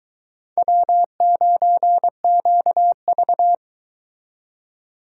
Morse code, heard as a keyed tone, about 23 wpm, 700 Hz. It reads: W9QV